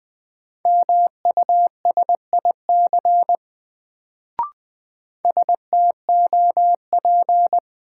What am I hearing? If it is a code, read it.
Morse code, heard as a keyed tone, about 21 wpm, 700 Hz. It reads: MUSIC STOP
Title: Sentences from Top 300 Words - New Word Drill - 20wpm